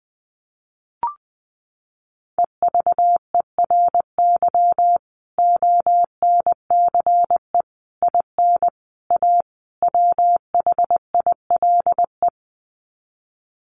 Morse code, heard as a keyed tone, about 20 wpm, 700 Hz. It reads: EVERY ONCE IN A WHILE
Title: Sets of 5 Words 20wpm